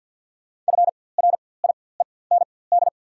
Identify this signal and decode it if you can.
Morse code, heard as a keyed tone, about 54 wpm, spaced as for 24 wpm, 700 Hz. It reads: FRIEND
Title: 4x Character Spacing - Top 300 Words - 50wpm